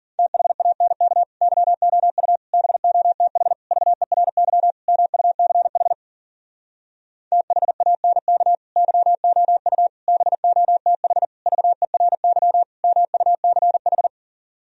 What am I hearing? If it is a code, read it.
Morse code, heard as a keyed tone, about 51 wpm, 700 Hz. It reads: THANK YOU BOTH VERY MUCH THANK YOU BOTH VERY MUCH